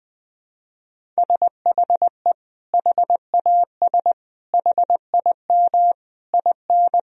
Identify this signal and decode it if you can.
Morse code, heard as a keyed tone, about 20 wpm, 700 Hz. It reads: SHE HAS HIM IN
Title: Sentences from Top 1000 Words - Review Part 4 - 20wpm